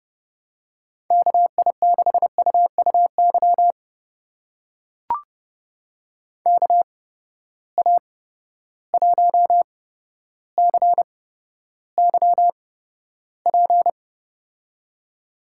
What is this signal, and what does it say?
Morse code, heard as a keyed tone, about 30 wpm, spaced as for 13 wpm, 700 Hz. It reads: KI6UUY KA1CYP
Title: USA Call Signs - 8x Character Spacing - 30wpm